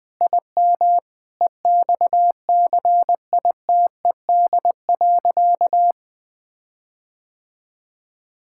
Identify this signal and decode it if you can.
Morse code, heard as a keyed tone, about 19 wpm, 700 Hz. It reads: IM EXCITED.
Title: Sentences with 6-Word Phrases - 20wpm